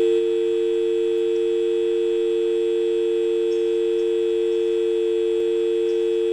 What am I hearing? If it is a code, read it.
Dial tone (UK), 350 + 450 Hz.